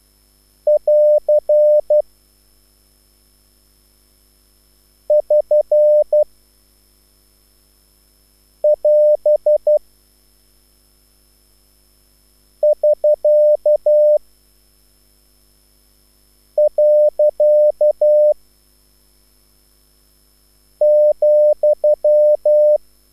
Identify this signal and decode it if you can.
Morse code, heard as a keyed tone, about 12 wpm, 600 Hz. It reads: <AR> <SN> <AS> <SK> . ,